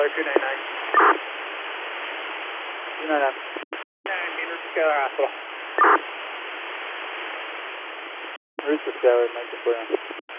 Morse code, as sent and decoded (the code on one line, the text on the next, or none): none